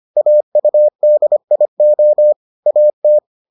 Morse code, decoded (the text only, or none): AUDIO AT